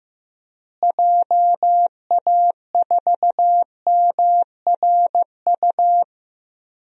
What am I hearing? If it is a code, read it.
Morse code, heard as a keyed tone, about 15 wpm, 700 Hz. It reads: JA4MRU
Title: DXCC-100 Call Signs - 6 Characters - Repeated 2x - 15wpm